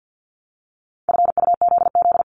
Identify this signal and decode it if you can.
Morse code, heard as a keyed tone, about 50 wpm, 700 Hz. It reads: FUZZ
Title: Multi-letter Phonemes 50wpm